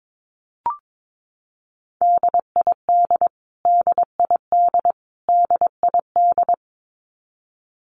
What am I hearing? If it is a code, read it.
Morse code, heard as a keyed tone, about 22 wpm, 700 Hz. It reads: DID DID DID
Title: Top 300 Words Repeated 22wpm